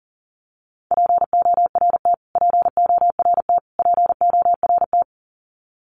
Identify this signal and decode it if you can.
Morse code, heard as a keyed tone, about 40 wpm, 700 Hz. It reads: PORT PORT PORT
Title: Top 300 Words Repeated 40wpm